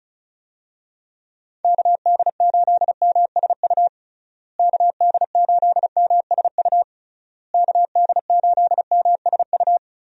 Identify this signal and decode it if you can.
Morse code, heard as a keyed tone, about 36 wpm, 700 Hz. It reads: KD8MSU KD8MSU KD8MSU
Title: USA Call Signs - 6 Characters - Repeated 3x - 35wpm